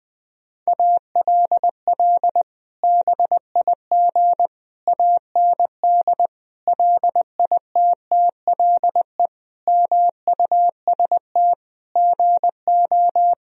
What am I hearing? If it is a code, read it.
Morse code, heard as a keyed tone, about 20 wpm, 700 Hz. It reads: ALL BIG AND LITTLE MUST GO
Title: Sentences from Top 1000 Words - Review Part 1 - 20wpm